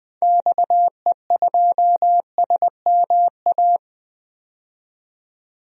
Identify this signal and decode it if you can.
Morse code, heard as a keyed tone, about 20 wpm, 700 Hz. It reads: XE2SMA
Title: DXCC-100 Call Signs - 6 Characters - 20wpm